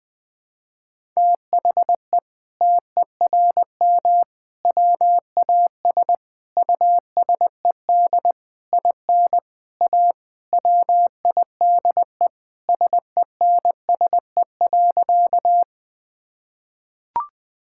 Morse code, heard as a keyed tone, about 20 wpm, 700 Hz. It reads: THE TERM WAS USED IN A WIDE SENSE.
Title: Sentences from Top 1000 Words - Review Part 6 - 20wpm